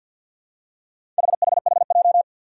Morse code, heard as a keyed tone, about 50 wpm, 700 Hz. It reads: HHHJ
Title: Four Challenging Letter-Number Combinations 50wpm